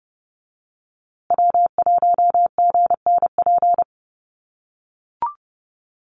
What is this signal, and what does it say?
Morse code, heard as a keyed tone, about 29 wpm, 700 Hz. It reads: W1GNP
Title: USA Call Signs Repeated 3x 30wpm